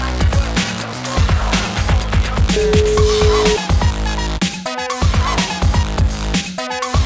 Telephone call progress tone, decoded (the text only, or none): ringback tone